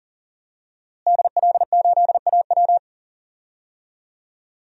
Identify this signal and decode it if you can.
Morse code, heard as a keyed tone, about 40 wpm, 700 Hz. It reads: DL8AW
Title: DXCC-100 Call Signs - 5 Characters - Repeated 2x - 40wpm